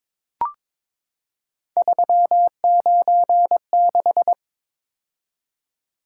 Morse code, heard as a keyed tone, about 22 wpm, 700 Hz. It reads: 396
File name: practice / 22wpm / Three Letter-Number Combinations - 22wpm